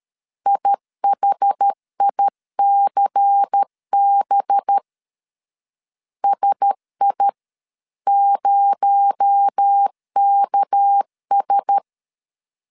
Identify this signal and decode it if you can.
Morse code, heard as a keyed tone, about 13 wpm, 800 Hz. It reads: IHICB SI0KS
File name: CW / DET / 1 6 1500C S A 12wpm 3440